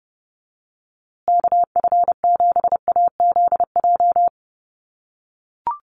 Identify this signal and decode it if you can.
Morse code, heard as a keyed tone, about 30 wpm, 700 Hz. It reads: KF7AZJ
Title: USA Call Signs Repeated 3x 30wpm